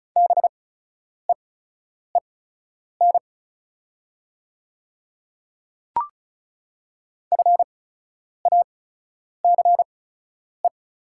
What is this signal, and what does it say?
Morse code, heard as a keyed tone, about 36 wpm, spaced as for 10 wpm, 690 Hz. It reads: BEEN FACE
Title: 8x Character Spacing - Top 300 Words - 35wpm